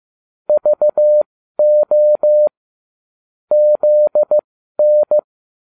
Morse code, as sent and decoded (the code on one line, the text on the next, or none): ...- --- / --.. -.
VO ZN